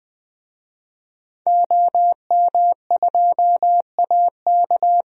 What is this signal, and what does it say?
Morse code, heard as a keyed tone, about 20 wpm, 700 Hz. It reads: OM2AK
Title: DXCC-100 Call Signs - 5 Characters - Repeated 2x - 20wpm